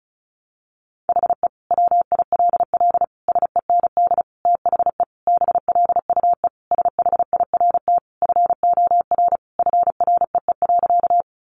Morse code, heard as a keyed tone, about 34 wpm, 700 Hz. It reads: HE WILL SEND THE BLUE SHIRT FOR FREE.
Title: Additional Easy Phrases 35wpm